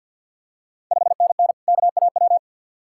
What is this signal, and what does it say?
Morse code, heard as a keyed tone, about 53 wpm, 690 Hz. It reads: 5NN KAW
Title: Ontario QSO Party Exchange 50wpm